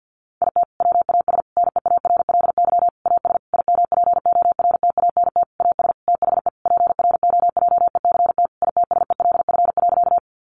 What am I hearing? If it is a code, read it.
Morse code, heard as a keyed tone, about 50 wpm, 700 Hz. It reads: IT WAS NEARLY AS IMPORTANT AS THE PROJECT ITSELF.
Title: Sentences from Top 1400 Words 50wpm